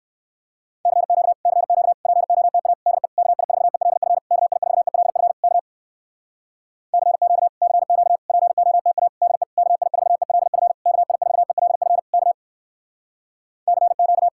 Morse code, heard as a keyed tone, about 58 wpm, 700 Hz. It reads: CQ CQ POTA DE KI4ELF KI4ELF K CQ CQ POTA DE KI4ELF KI4ELF K CQ